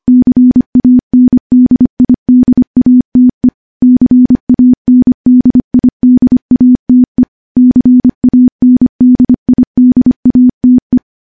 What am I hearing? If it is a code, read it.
Morse code, heard as a keyed tone, about 25 wpm, 250 Hz. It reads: CANDIDATE CANDIDATE CANDIDATE
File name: morse / morse-training-25WPM